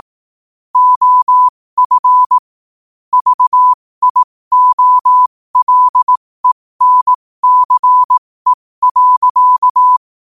Morse code, heard as a keyed tone, about 18 wpm, spaced as for 15 wpm, 1000 Hz. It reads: OF VIOLENCE.
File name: qotdfiles / 15wpm